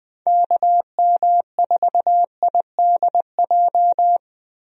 Morse code, heard as a keyed tone, about 20 wpm, 700 Hz. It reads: KM4IDJ
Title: USA Call Signs - 6 Characters - Repeated 3x - 20wpm